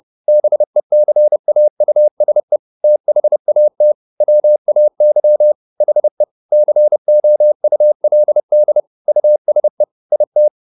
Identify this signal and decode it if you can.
Morse code, heard as a keyed tone, about 30 wpm, 600 Hz. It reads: BECAUSE THAT WAY HE COULD USE IT